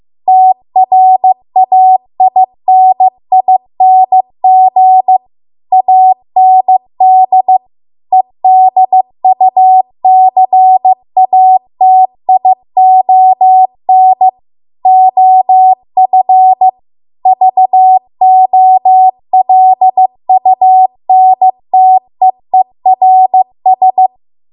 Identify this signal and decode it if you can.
Morse code, heard as a keyed tone, about 15 wpm, 750 Hz. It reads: TRAINING AND EDUCATION OF VOLUNTEERS